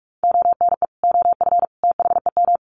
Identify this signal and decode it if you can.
Morse code, heard as a keyed tone, about 44 wpm, 700 Hz. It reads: ONE OF THEM
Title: Sets of 3 Words 45wpm